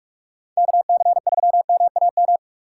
Morse code, heard as a keyed tone, about 58 wpm, 700 Hz. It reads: KC3MAM